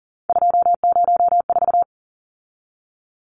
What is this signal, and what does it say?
Morse code, heard as a keyed tone, about 40 wpm, 700 Hz. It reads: 204
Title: Numbers - 3-Digit 40wpm